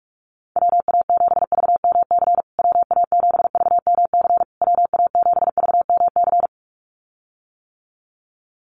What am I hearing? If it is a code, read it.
Morse code, heard as a keyed tone, about 46 wpm, 700 Hz. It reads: WA7VMC WA7VMC WA7VMC
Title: USA Call Signs Repeated 3x 45wpm